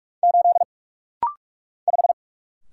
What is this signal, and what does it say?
Morse code, heard as a keyed tone, about 45 wpm, 700 Hz. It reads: 8 5
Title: Single Letter-Number - Rapid-Fire 45wpm